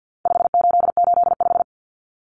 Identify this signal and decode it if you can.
Morse code, heard as a keyed tone, about 48 wpm, 700 Hz. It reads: 5885
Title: Numbers - 4-Digit 50wpm